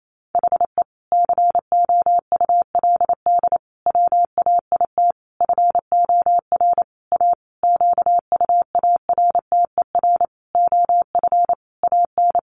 Morse code, heard as a keyed tone, about 28 wpm, 700 Hz. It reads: HE COULD WAIT FOR A QUARTER OF AN
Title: Sentences from Top 1300 Words 28wpm